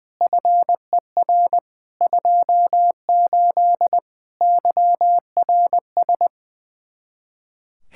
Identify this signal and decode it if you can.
Morse code, heard as a keyed tone, about 20 wpm, 700 Hz. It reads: FER 28 YRS
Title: Practice QSO - Exchange 3 - More Info - 20wpm